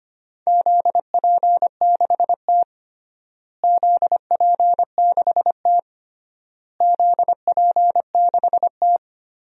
Morse code, heard as a keyed tone, about 25 wpm, 700 Hz. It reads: ZP6T ZP6T ZP6T
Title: DXCC-100 Call Signs - 4 Characters - Repeated 3x - 25wpm